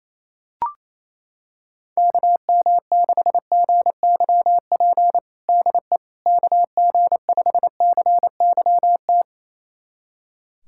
Morse code, heard as a keyed tone, about 27 wpm, 700 Hz. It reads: KM6GYP DE KG5CYT
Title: USA Callsign DE Callsign - 6 Characters - 28wpm